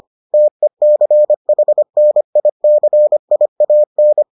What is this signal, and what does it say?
Morse code, heard as a keyed tone, about 25 wpm, 600 Hz. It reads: TECHNICIAN